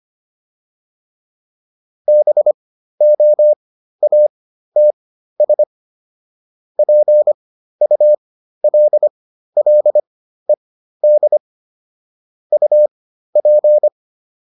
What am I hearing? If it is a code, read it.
Morse code, heard as a keyed tone, about 25 wpm, spaced as for 13 wpm, 600 Hz. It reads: BOATS PULLED UP